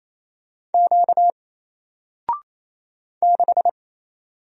Morse code, heard as a keyed tone, about 28 wpm, 700 Hz. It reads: Q 6
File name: Single-Letter-Number-28wpm